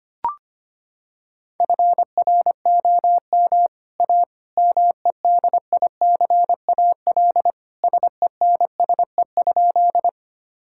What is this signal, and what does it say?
Morse code, heard as a keyed tone, about 25 wpm, 700 Hz. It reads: FROM A MEDICAL SENSE?
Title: Sentences from Top 1000 Words - Review Part 5 - 25wpm